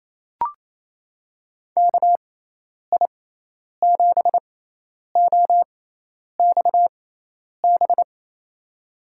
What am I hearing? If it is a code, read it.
Morse code, heard as a keyed tone, about 28 wpm, spaced as for 10 wpm, 700 Hz. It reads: KI7OXB
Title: USA Call Signs - 6 Characters - 6x Character Spacing - 28wpm